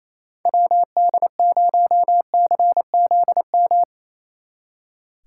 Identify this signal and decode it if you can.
Morse code, heard as a keyed tone, about 35 wpm, 700 Hz. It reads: WD0CZM